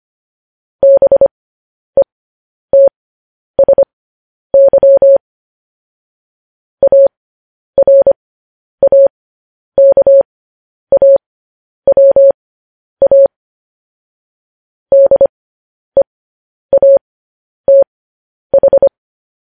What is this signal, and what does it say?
Morse code, heard as a keyed tone, about 25 wpm, spaced as for 10 wpm, 550 Hz. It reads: BETSY ARAKAWA DEATH
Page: News Headlines in Morse Code at 10 WPM